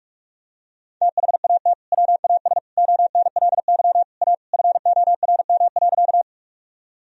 Morse code, heard as a keyed tone, about 41 wpm, 700 Hz. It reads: THAT WAS ONLY A FORM.
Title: Sentences from Top 1000 Words - Review Part 1 - 45wpm